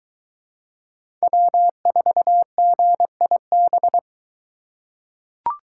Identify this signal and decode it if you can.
Morse code, heard as a keyed tone, about 23 wpm, 700 Hz. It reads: W4GIB